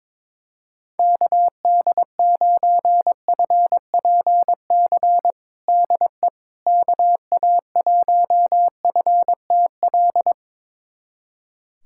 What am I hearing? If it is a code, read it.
Morse code, heard as a keyed tone, about 22 wpm, 700 Hz. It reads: KD9FPC DE KA1FTL
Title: USA Callsign DE Callsign - 6 Characters - 22wpm